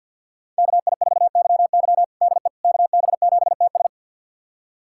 Morse code, heard as a keyed tone, about 50 wpm, 700 Hz. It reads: KI4YY DE KB7TS
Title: USA Callsign DE Callsign - 5 Characters - 50wpm